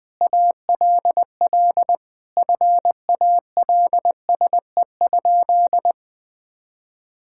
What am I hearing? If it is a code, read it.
Morse code, heard as a keyed tone, about 21 wpm, 700 Hz. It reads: ALL FALSE?
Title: Sentences from Top 1600 Words 20wpm